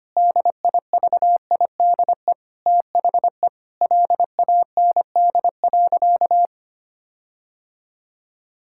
Morse code, heard as a keyed tone, about 25 wpm, 700 Hz. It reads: DIVIDE THE LAND.
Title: Sentences from Top 1600 Words 25wpm